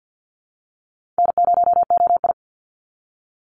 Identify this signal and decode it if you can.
Morse code, heard as a keyed tone, about 50 wpm, 700 Hz. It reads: N0OI
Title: NA SOTA Chaser Callsigns 50wpm